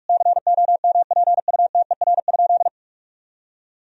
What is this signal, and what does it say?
Morse code, heard as a keyed tone, about 46 wpm, 700 Hz. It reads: COMPUTER?
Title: Sentences from Top 1000 Words - Review Part 3 - 45wpm